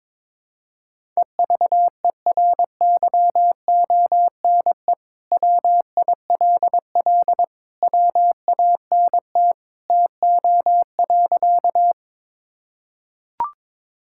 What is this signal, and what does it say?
Morse code, heard as a keyed tone, about 22 wpm, 700 Hz. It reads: EVERYONE WILL WANT TO.
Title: Sentences from Top 1000 Words - Review Part 5 - 22wpm